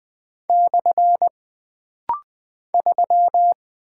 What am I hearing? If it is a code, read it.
Morse code, heard as a keyed tone, about 20 wpm, 700 Hz. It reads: / 3
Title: Single Character (Letter:Number:Punctuation:Prosign) - Rapid-Fire - 20wpm